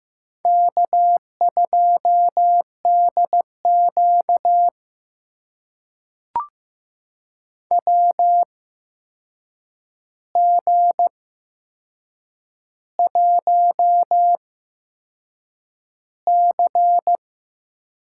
Morse code, heard as a keyed tone, about 15 wpm, spaced as for 6 wpm, 700 Hz. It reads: K2DQ WG1C
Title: USA Call Signs - 8x Character Spacing - 15wpm